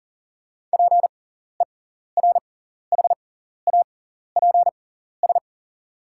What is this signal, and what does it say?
Morse code, heard as a keyed tone, about 41 wpm, spaced as for 14 wpm, 700 Hz. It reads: PERHAPS
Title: 6x Character Spacing - Top 500 Words - 40wpm